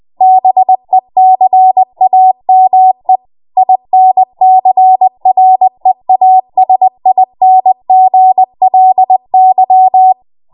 Morse code, heard as a keyed tone, about 20 wpm, 750 Hz. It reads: BECAME INCREASINGLY
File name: Morse / 20WPM